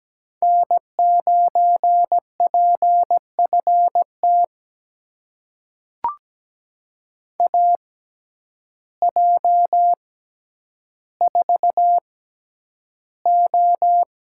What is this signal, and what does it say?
Morse code, heard as a keyed tone, about 17 wpm, spaced as for 9 wpm, 700 Hz. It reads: N9PFT AJ4O
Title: USA Call Signs - 5 Characters - 6x Character Spacing - 17wpm